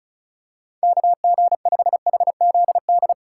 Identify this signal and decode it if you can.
Morse code, heard as a keyed tone, about 36 wpm, 700 Hz. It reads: KG5HZD
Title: USA Call Signs - 6 Characters - 4x Character Spacing - 35wpm